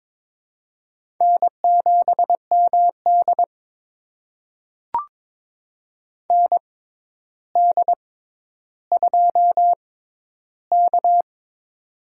Morse code, heard as a keyed tone, about 22 wpm, spaced as for 11 wpm, 700 Hz. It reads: N7MD ND2K
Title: USA Call Signs - 4 Characters - 6x Character Spacing - 22wpm